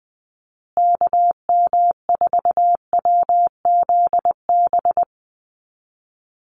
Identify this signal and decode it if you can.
Morse code, heard as a keyed tone, about 20 wpm, 700 Hz. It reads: KM4WZB
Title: USA Call Signs Repeated 3x 20wpm